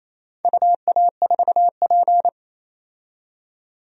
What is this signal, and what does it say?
Morse code, heard as a keyed tone, about 27 wpm, 700 Hz. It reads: UA4P
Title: DXCC-100 Call Signs - 4 Characters - 28wpm